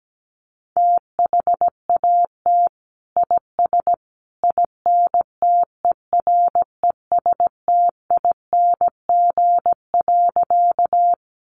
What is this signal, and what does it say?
Morse code, heard as a keyed tone, about 17 wpm, 700 Hz. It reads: THAT IS INTERESTING.
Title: Sentences from Top 1000 Words 17wpm